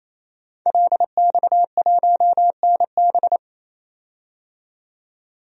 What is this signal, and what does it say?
Morse code, heard as a keyed tone, about 28 wpm, 700 Hz. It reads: LX1NB
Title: DXCC-100 Call Signs - 5 Characters - 28wpm